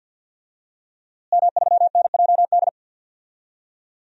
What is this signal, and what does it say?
Morse code, heard as a keyed tone, about 51 wpm, 700 Hz. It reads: M3NJD
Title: DXCC-100 Call Signs - 5 Characters - Repeated 2x - 50wpm